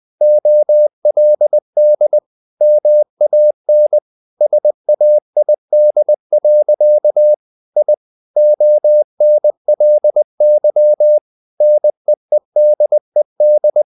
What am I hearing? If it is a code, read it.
Morse code, heard as a keyed tone, about 20 wpm, 600 Hz. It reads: OLD MAN SAID. I ONLY NEEDED
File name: the old man and the sea / the old man and the sea part 1 20wpm